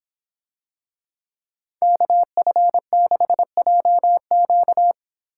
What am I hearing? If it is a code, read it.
Morse code, heard as a keyed tone, about 25 wpm, 700 Hz. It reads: KF6JQ